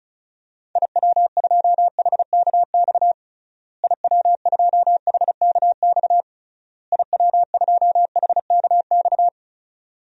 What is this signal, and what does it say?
Morse code, heard as a keyed tone, about 36 wpm, 700 Hz. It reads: IW2HKX IW2HKX IW2HKX
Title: DXCC-100 Call Signs - 6 Characters - Repeated 3x - 35wpm